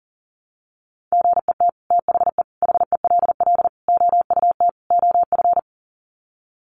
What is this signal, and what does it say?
Morse code, heard as a keyed tone, about 40 wpm, 700 Hz. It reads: GET THE HELL OUT OF
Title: Sets of 5 Words Encore2 40wpm